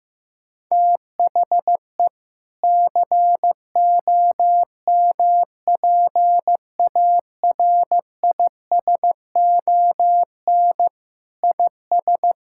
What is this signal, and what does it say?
Morse code, heard as a keyed tone, about 15 wpm, 700 Hz. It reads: THE COMPARISON IS
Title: Sentences from Top 1600 Words 15wpm